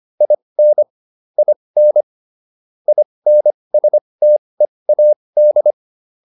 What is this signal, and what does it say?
Morse code, heard as a keyed tone, about 25 wpm, spaced as for 20 wpm, 600 Hz. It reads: IN IN INSTEAD